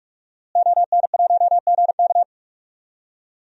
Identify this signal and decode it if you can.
Morse code, heard as a keyed tone, about 46 wpm, 700 Hz. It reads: ON1GK
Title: DXCC-100 Call Signs - 5 Characters - Repeated 2x - 45wpm